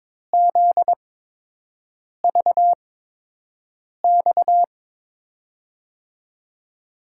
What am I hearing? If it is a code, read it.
Morse code, heard as a keyed tone, about 21 wpm, spaced as for 6 wpm, 700 Hz. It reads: ZVX
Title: USA Call Signs - 8x Character Spacing - 22wpm